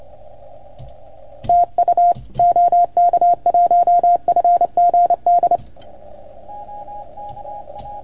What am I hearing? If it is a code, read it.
Morse code, heard as a keyed tone, about 26 wpm, 690 Hz. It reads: TU OK1FGD
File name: Archiv / CUC310 / OK1FGD CUC 310 pileup